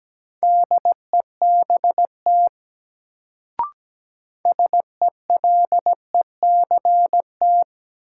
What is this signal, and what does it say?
Morse code, heard as a keyed tone, about 17 wpm, 700 Hz. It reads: DEBT SELECT
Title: Sentences from Top 1300 Words - New Word Drill - 17wpm